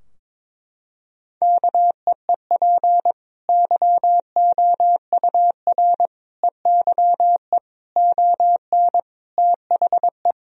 Morse code, heard as a keyed tone, about 22 wpm, 700 Hz. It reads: KEEP YOUR EYE ON THE
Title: Sentences from Top 1000 Words - Review Part 4 - 22wpm